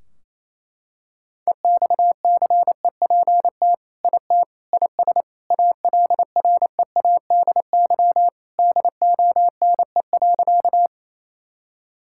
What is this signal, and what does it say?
Morse code, heard as a keyed tone, about 28 wpm, 700 Hz. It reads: EXCEPT IT IS ALREADY DONE.
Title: Sentences from Top 1600 Words 28wpm